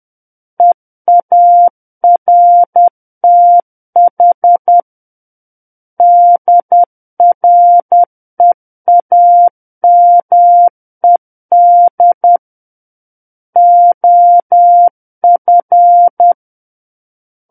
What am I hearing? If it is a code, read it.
Morse code, heard as a keyed tone, about 10 wpm, 700 Hz. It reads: EARTH DREAMED OF
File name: Book One / 01-Chapter 1 - 10 WPM